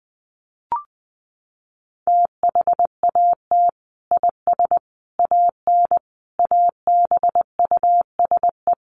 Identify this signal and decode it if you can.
Morse code, heard as a keyed tone, about 20 wpm, 700 Hz. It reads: THAT IS AN ABUSE
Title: Sentences from Top 1300 Words 20wpm